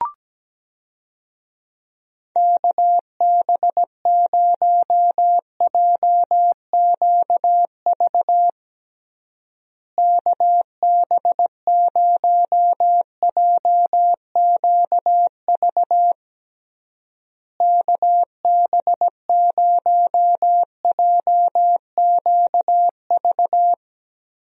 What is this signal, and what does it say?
Morse code, heard as a keyed tone, about 17 wpm, 700 Hz. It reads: KB0JQV KB0JQV KB0JQV
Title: USA Call Signs - 6 Characters - Repeated 3x - 17wpm